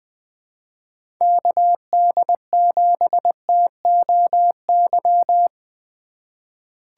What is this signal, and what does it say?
Morse code, heard as a keyed tone, about 20 wpm, 700 Hz. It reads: KD7TOY
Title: USA Call Signs - 4x Character Spacing - 20wpm